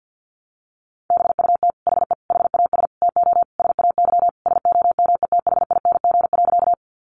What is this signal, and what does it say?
Morse code, heard as a keyed tone, about 49 wpm, 700 Hz. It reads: BUT HE HAS TO SAY SOMETHING.
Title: Sentences from Top 600 Words 50wpm